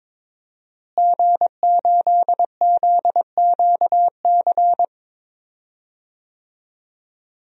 Morse code, heard as a keyed tone, about 21 wpm, 700 Hz. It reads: G8ZQC
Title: DXCC-100 Call Signs - 5 Characters - Repeated 3x - 22wpm